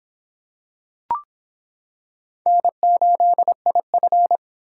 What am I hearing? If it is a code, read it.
Morse code, heard as a keyed tone, about 25 wpm, 700 Hz. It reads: N8IF